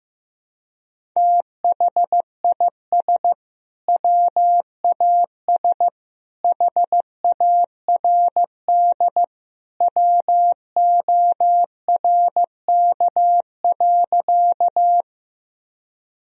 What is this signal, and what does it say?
Morse code, heard as a keyed tone, about 15 wpm, 700 Hz. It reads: THIS WAS HARD WORK.
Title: Sentences from Top 1000 Words - Review Part 1 - 15wpm